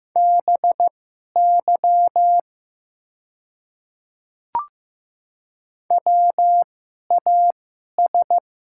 Morse code, heard as a keyed tone, about 15 wpm, spaced as for 11 wpm, 700 Hz. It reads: BY WAS
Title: 2x Character Spacing - Top 100 Words - 15wpm